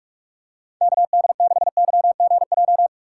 Morse code, heard as a keyed tone, about 46 wpm, 700 Hz. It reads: KD6YGJ
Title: USA Call Signs - 6 Characters - 2x Character Spacing - 45wpm